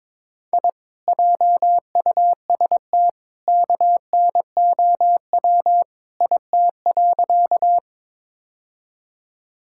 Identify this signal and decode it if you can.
Morse code, heard as a keyed tone, about 22 wpm, 700 Hz. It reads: I JUST KNOW IT.
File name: practice / 22wpm / Sentences with 2-Word Phrases - 22wpm